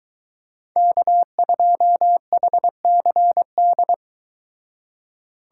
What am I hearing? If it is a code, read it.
Morse code, heard as a keyed tone, about 23 wpm, 700 Hz. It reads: K2HCD